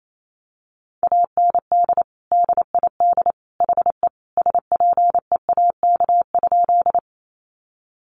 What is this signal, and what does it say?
Morse code, heard as a keyed tone, about 28 wpm, 700 Hz. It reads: AND DID HE SPEAK?
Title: Sentences from Top 1100 Words 28wpm